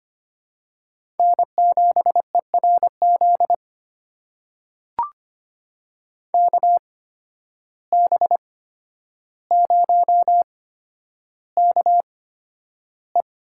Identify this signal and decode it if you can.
Morse code, heard as a keyed tone, about 25 wpm, spaced as for 11 wpm, 700 Hz. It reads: N7ERZ KB0KE
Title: USA Call Signs - 5 Characters - 8x Character Spacing - 25wpm